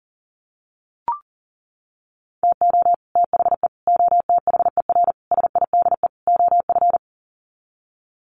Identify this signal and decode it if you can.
Morse code, heard as a keyed tone, about 40 wpm, 700 Hz. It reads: TO THE OTHER SIDE OF
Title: Sets of 5 Words Encore 40wpm